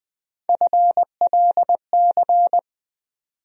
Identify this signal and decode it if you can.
Morse code, heard as a keyed tone, about 21 wpm, 700 Hz. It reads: FLC
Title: USA Call Sign Postfixes - 3 Letters 20wpm